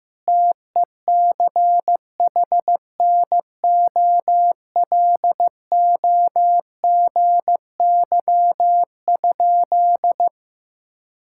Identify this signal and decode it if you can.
Morse code, heard as a keyed tone, about 15 wpm, 700 Hz. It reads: TECHNOLOGY?
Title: Sentences from Top 1000 Words - Review Part 5 - 15wpm